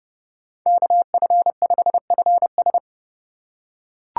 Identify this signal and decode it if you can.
Morse code, heard as a keyed tone, about 31 wpm, 700 Hz. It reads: KF5FS
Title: USA Call Signs - 3x Character Spacing - 30wpm